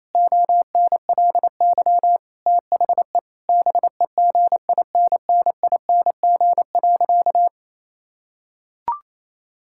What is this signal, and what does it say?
Morse code, heard as a keyed tone, about 28 wpm, 700 Hz. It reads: ONLY THE BEGINNING.
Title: Sentences from Top 1000 Words - Review Part 5 - 28wpm